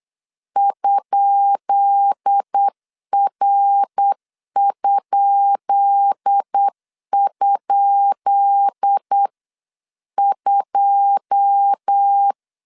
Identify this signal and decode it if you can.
Morse code, heard as a keyed tone, about 8 wpm, 800 Hz. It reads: ?R??2
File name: LEZIONI / 10 SIGNS / 10B INT 8wpm 3534